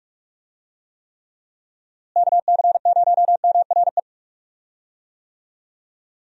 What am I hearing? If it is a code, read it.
Morse code, heard as a keyed tone, about 43 wpm, 700 Hz. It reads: KC0MRE
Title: USA Call Signs - 6 Characters - Repeated 2x - 45wpm